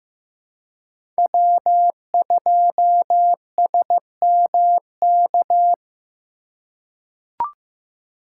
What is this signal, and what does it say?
Morse code, heard as a keyed tone, about 15 wpm, 700 Hz. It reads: W2SMK
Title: USA Call Signs - 5 Characters - 3x Character Spacing - 15wpm